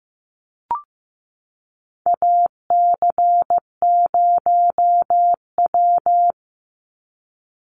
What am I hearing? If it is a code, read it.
Morse code, heard as a keyed tone, about 15 wpm, 700 Hz. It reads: AC0W
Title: Call Signs Repeated 2x - 15wpm